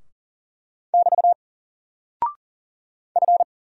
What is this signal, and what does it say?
Morse code, heard as a keyed tone, about 40 wpm, 700 Hz. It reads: <BT> F
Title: Single Character (Letter:Number:Punctuation:Prosign) Alt 40wpm